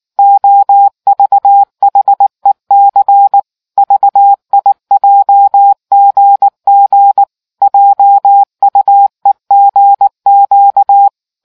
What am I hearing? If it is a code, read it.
Morse code, heard as a keyed tone, about 19 wpm, 800 Hz. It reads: OVHEC VIJGG JUEGQ